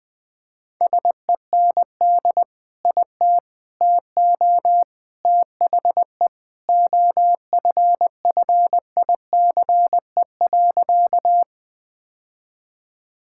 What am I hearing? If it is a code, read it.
Morse code, heard as a keyed tone, about 19 wpm, 700 Hz. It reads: SEND IT TO THE OFFICE.
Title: Sentences with 3-Word Phrases - Spoken Beginning - 20wpm